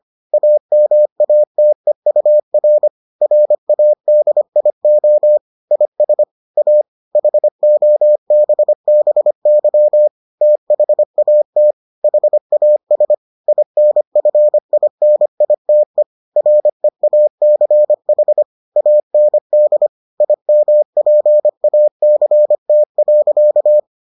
Morse code, heard as a keyed tone, about 25 wpm, 600 Hz. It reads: AMATEUR RADIO IS A HOBBY THAT HAS INFINITE REACH AND IMPACT.